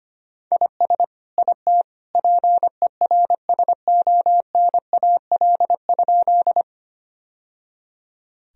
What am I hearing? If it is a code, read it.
Morse code, heard as a keyed tone, about 25 wpm, 700 Hz. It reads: IS IT PERSONAL?
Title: Sentences from Top 1000 Words - Review Part 4 - 25wpm